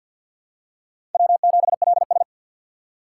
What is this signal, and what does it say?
Morse code, heard as a keyed tone, about 47 wpm, 700 Hz. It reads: W7LS
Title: USA Call Signs - 4 Characters - 3x Character Spacing - 50wpm